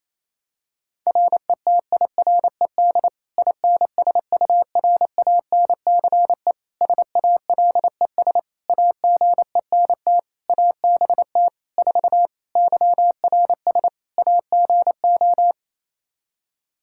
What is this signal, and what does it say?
Morse code, heard as a keyed tone, about 28 wpm, 700 Hz. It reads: RETIRED INSURANCE SALES AGENT ABT 4 YRS AGO
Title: K1USN SST QSO Practice 28wpm